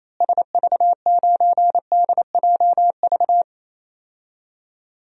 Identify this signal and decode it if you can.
Morse code, heard as a keyed tone, about 28 wpm, 700 Hz. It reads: SV9DJV
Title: DXCC-100 Call Signs - 6 Characters - Repeated 2x - 28wpm